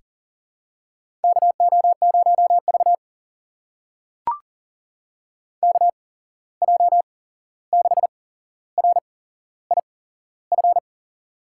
Morse code, heard as a keyed tone, about 40 wpm, spaced as for 15 wpm, 700 Hz. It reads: KO0V KJ6RIF
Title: USA Call Signs - 8x Character Spacing - 40wpm